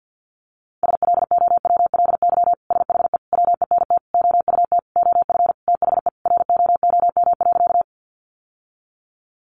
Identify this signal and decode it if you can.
Morse code, heard as a keyed tone, about 50 wpm, 700 Hz. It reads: SLOWLY SHE WENT OUT OF THE ROOM.
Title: Sentences from Top 1100 Words 50wpm